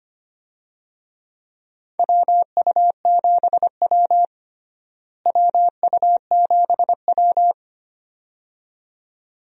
Morse code, heard as a keyed tone, about 25 wpm, 700 Hz. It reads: WU7W WU7W
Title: USA Call Signs - 4 Characters - Repeated 2x - 25wpm